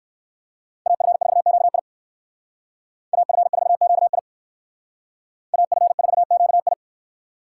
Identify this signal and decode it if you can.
Morse code, heard as a keyed tone, about 58 wpm, 700 Hz. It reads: AF4QI AF4QI AF4QI